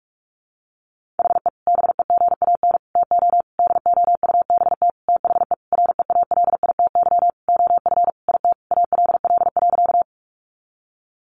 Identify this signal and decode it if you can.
Morse code, heard as a keyed tone, about 44 wpm, 700 Hz. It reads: HE BEGAN TO DOUBT THE REALITY OF IT ALL.
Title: Sentences from Top 1500 Words 45wpm